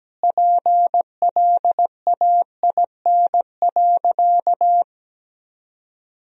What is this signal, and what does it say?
Morse code, heard as a keyed tone, about 17 wpm, 700 Hz. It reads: PLAIN.
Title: 3x Word-Spacing - Sentences from Top 400 Words - 17wpm